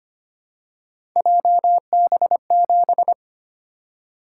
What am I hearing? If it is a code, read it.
Morse code, heard as a keyed tone, about 25 wpm, 700 Hz. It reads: JB7
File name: Three Letter-Number Combinations - 25wpm